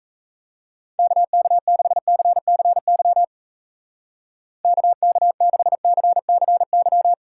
Morse code, heard as a keyed tone, about 34 wpm, 700 Hz. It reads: KK6CCY KK6CCY
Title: Speed Racing - USA 2x3 Call Signs 28wpm